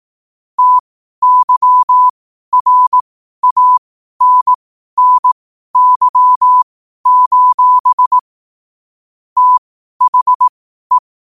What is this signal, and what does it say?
Morse code, heard as a keyed tone, about 18 wpm, spaced as for 12 wpm, 1000 Hz. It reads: TYRANNY: THE
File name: 12wpm